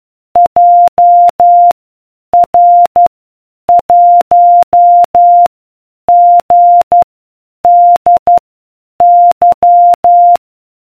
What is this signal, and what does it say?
Morse code, heard as a keyed tone, about 12 wpm, spaced as for 8 wpm, 700 Hz. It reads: JR1GDY